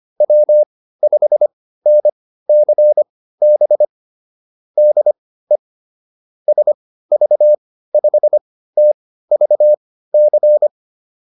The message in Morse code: W5NCB DE SV5TVC

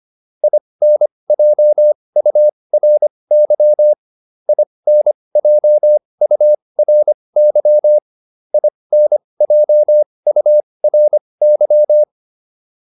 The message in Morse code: INJURY INJURY INJURY

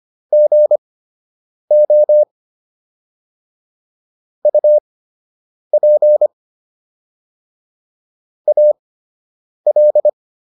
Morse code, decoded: GO UP AL

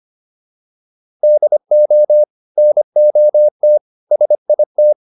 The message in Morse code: DO NOT SIT

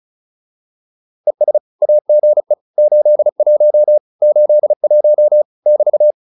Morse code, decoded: ES AGE 81 81 <BT>

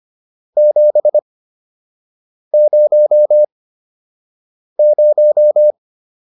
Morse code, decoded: 700

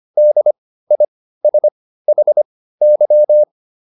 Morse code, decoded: DISHY

